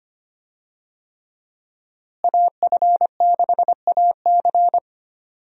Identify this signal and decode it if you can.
Morse code, heard as a keyed tone, about 25 wpm, 700 Hz. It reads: AF6AC
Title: USA Call Signs - 5 Characters - Repeated 3x - 25wpm